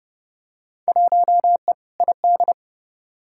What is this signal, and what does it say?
Morse code, heard as a keyed tone, about 30 wpm, 700 Hz. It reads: 1E ID
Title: Field Day Exchanges - 30wpm